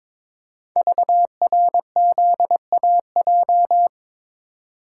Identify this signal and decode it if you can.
Morse code, heard as a keyed tone, about 22 wpm, 690 Hz. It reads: VRZAJ
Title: Code Groups - 5 Letters - 22wpm